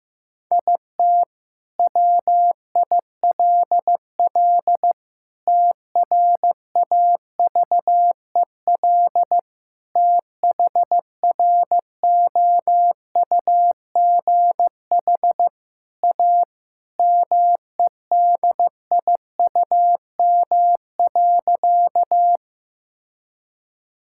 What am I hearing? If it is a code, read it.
Morse code, heard as a keyed tone, about 15 wpm, 700 Hz. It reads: IT WILL TRAVEL THROUGH A MEDIUM.
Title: Sentences from Top 1000 Words - Review Part 6 - 15wpm